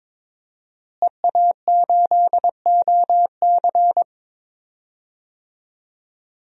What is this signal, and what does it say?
Morse code, heard as a keyed tone, about 22 wpm, 700 Hz. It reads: EA8OC
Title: DXCC-100 Call Signs - 5 Characters - Repeated 3x - 22wpm